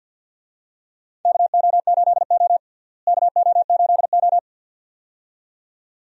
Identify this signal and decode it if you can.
Morse code, heard as a keyed tone, about 50 wpm, 700 Hz. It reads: KO8O KO8O
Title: USA Call Signs - 4 Characters - Repeated 2x - 50wpm